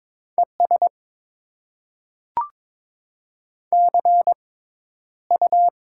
Morse code, heard as a keyed tone, about 22 wpm, spaced as for 11 wpm, 690 Hz. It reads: ES CU
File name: Top 20 QSO Elements - 6x Character Spacing - 22wpm